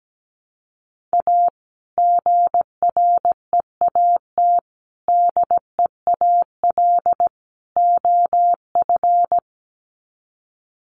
Morse code, heard as a keyed tone, about 17 wpm, 700 Hz. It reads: A GREAT DEAL OF
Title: Sets of 4 Words 17wpm